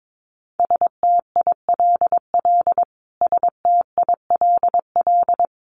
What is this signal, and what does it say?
Morse code, heard as a keyed tone, about 22 wpm, 700 Hz. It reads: STILL STILL
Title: Top 500 Words Repeated 22wpm